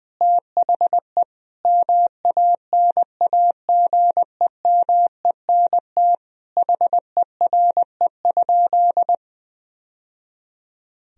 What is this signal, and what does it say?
Morse code, heard as a keyed tone, about 21 wpm, 700 Hz. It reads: THE MANAGEMENT HERE?
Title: Sentences from Top 1000 Words - Review Part 4 - 20wpm